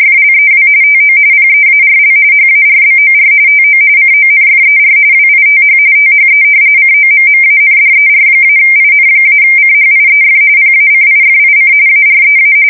RTTY, about 45.45 baud. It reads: FLEW TO PEARL HARBOR TO CONFER WITH THE COMMANDER-IN-CHIEF PACIFIC AREA
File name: fsk170 / battleshipnj